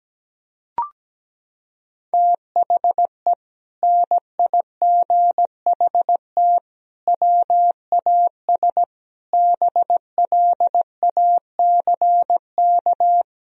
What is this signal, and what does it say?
Morse code, heard as a keyed tone, about 17 wpm, 700 Hz. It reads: THE NIGHT WAS BLACK
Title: Sentences from Top 1000 Words - Review Part 2 - 17wpm